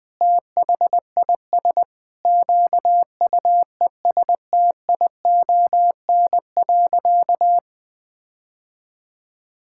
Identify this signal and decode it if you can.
Morse code, heard as a keyed tone, about 20 wpm, 700 Hz. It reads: THIS QUESTION.
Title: Sentences with 5-Word Phrases - 20wpm